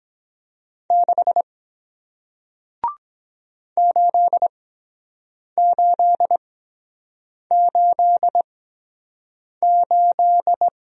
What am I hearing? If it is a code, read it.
Morse code, heard as a keyed tone, about 23 wpm, spaced as for 7 wpm, 700 Hz. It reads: 6 8888